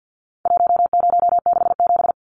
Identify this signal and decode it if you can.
Morse code, heard as a keyed tone, about 50 wpm, 700 Hz. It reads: 1067
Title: Numbers - 4-Digit 50wpm